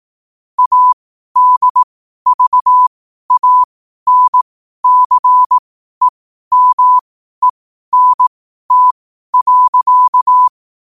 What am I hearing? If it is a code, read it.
Morse code, heard as a keyed tone, about 18 wpm, spaced as for 12 wpm, 1000 Hz. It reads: ADVANCEMENT.
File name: qotdfiles / 12wpm